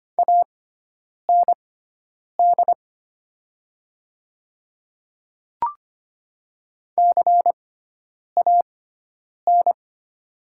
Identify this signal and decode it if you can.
Morse code, heard as a keyed tone, about 25 wpm, spaced as for 9 wpm, 700 Hz. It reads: AND CAN
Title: 6x Character Spacing - Top 200 Words - 25wpm